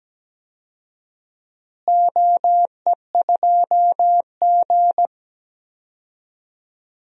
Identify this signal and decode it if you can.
Morse code, heard as a keyed tone, about 17 wpm, 700 Hz. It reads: OE2G
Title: DXCC-100 Call Signs - 4 Characters - Repeated 3x - 17wpm